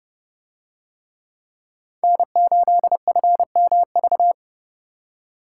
Morse code, heard as a keyed tone, about 29 wpm, 700 Hz. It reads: N8FMV